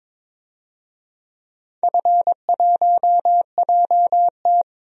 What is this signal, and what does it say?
Morse code, heard as a keyed tone, about 22 wpm, 700 Hz. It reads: F1JT
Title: DXCC-100 Call Signs - 4 Characters - 22wpm